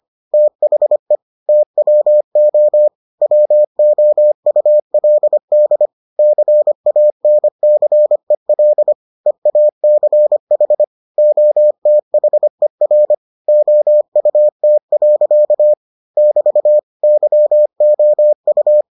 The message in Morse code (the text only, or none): THE TWO WOULD CANCEL EACH OTHER OUT. <BT> YOU